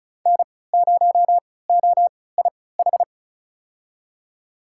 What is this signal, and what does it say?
Morse code, heard as a keyed tone, about 36 wpm, spaced as for 20 wpm, 700 Hz. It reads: N0OIH
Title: USA Call Signs - 5 Characters - 3x Character Spacing - 35wpm